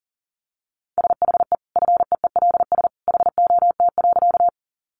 Morse code, heard as a keyed tone, about 40 wpm, 700 Hz. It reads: SHE FEELS HOT.
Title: Additional Easy Phrases 40wpm